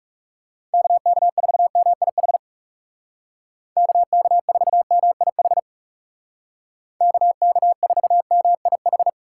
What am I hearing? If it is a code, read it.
Morse code, heard as a keyed tone, about 40 wpm, 700 Hz. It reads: KK4MIH KK4MIH KK4MIH